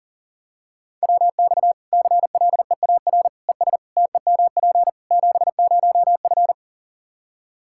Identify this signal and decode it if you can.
Morse code, heard as a keyed tone, about 37 wpm, 700 Hz. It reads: WX CLEAR ES TEMP 70F
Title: K1USN SST QSO Practice 40wpm